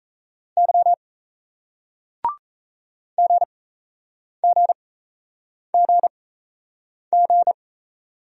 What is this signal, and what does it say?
Morse code, heard as a keyed tone, about 38 wpm, spaced as for 8 wpm, 700 Hz. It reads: Y GGGG